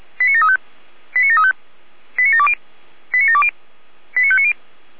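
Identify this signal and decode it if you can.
Five-tone selective call (ZVEI), heard as quick runs of five tones, tones of about 80 ms. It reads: 89735 89735 89720 89720 89590
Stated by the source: ZVEI five-tone call 89706, received on SDR.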